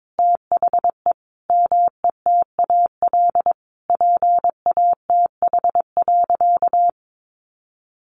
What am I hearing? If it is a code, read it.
Morse code, heard as a keyed tone, about 22 wpm, 700 Hz. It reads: THE METAL PATH.
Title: A Sound Of Thunder 22wpm